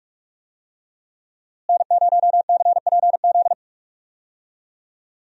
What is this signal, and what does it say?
Morse code, heard as a keyed tone, about 41 wpm, 700 Hz. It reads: N0CPZ
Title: USA Call Signs - 5 Characters - Repeated 3x - 45wpm